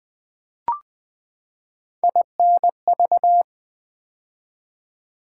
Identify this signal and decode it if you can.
Morse code, heard as a keyed tone, about 20 wpm, 700 Hz. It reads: INV
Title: Build A Word - 3 Letters At A Time - 6-Letter Words - 20wpm